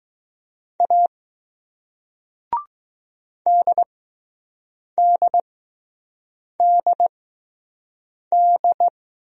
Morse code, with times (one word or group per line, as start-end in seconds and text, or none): 0.8-1.1 A
3.5-8.9 DDDD